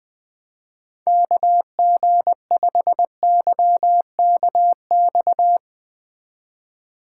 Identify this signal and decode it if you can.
Morse code, heard as a keyed tone, about 19 wpm, 700 Hz. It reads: KG5YKX
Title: Speed Racing - USA 2x3 Call Signs 20wpm